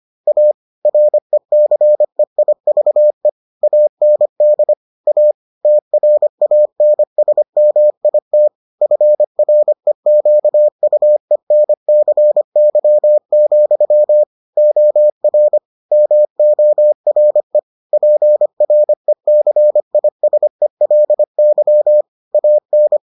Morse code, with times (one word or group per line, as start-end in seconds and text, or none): 0.3-0.5 A
0.9-3.3 RECEIVE
3.6-4.7 AND
5.1-5.3 A
5.6-8.5 TRANSMIT
8.8-14.2 FREQUENCY,
14.6-15.6 OR
15.9-17.6 MORE
17.9-22.0 PRECISELY
22.3-23.0 AN